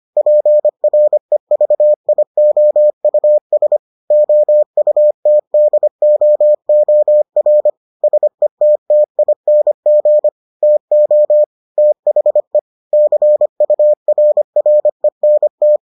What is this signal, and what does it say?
Morse code, heard as a keyed tone, about 25 wpm, 600 Hz. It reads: PREVIOUS OUTDOOR SETTING TO THE CURRENT